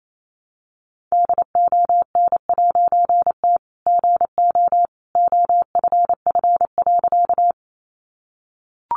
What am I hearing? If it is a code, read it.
Morse code, heard as a keyed tone, about 28 wpm, 700 Hz. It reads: DON'T GO OFF.
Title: A Sound Of Thunder 28wpm